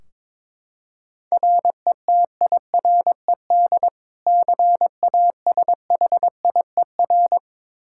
Morse code, heard as a keyed tone, about 21 wpm, 700 Hz. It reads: RETIRED CASHIER
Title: Practice QSO - Exchange 3 - More Info - 22wpm